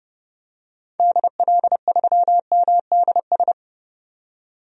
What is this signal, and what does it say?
Morse code, heard as a keyed tone, about 30 wpm, 700 Hz. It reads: DL3MDS
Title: DXCC-100 Call Signs - 6 Characters - 30wpm